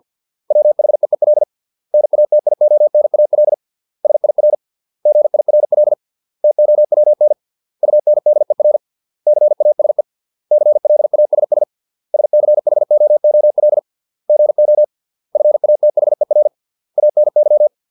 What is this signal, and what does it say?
Morse code, heard as a keyed tone, about 50 wpm, 600 Hz. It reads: WHEEL NATIONAL SIR GIRL TOWN UNDER CASE CLASS SCHOOL GO FATHER ANY